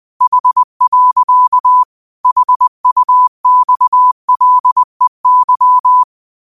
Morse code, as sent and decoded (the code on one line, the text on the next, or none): .... .-.-.- / .... ..- -..- .-.. -.--
H. HUXLEY